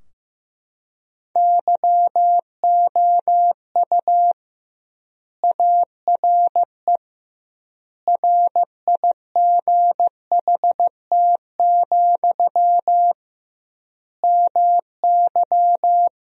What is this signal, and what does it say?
Morse code, heard as a keyed tone, about 15 wpm, 700 Hz. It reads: YOU ARE RIGHT, MY
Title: 2x Word-Spacing - Sentences from Top 400 Words - 15wpm